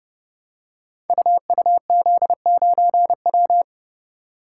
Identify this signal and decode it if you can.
Morse code, heard as a keyed tone, about 30 wpm, 700 Hz. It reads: UUZ9W
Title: Code Groups - 5 Letter:Numbers - 30wpm